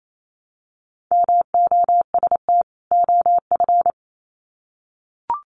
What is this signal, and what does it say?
Morse code, heard as a keyed tone, about 28 wpm, 700 Hz. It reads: MOST OF
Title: Sets of 2 Words 28wpm